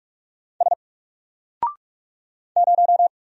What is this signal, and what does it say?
Morse code, heard as a keyed tone, about 44 wpm, 700 Hz. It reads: S 0
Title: Single Character (Letter:Number:Punctuation:Prosign) 45wpm